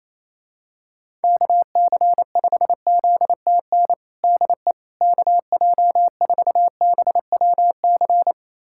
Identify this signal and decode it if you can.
Morse code, heard as a keyed tone, about 28 wpm, 700 Hz. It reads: KC5ZTN DE KJ4BWC
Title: USA Callsign DE Callsign 28wpm